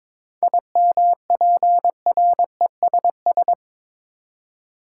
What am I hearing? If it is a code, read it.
Morse code, heard as a keyed tone, about 22 wpm, 700 Hz. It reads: IMPRESS
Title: Seven Letter Words - 22wpm